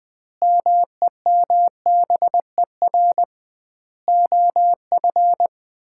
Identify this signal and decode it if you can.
Morse code, heard as a keyed tone, about 21 wpm, 700 Hz. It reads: MEMBER OF
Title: 2x Word-Spacing - Sets of 4 Words - 20wpm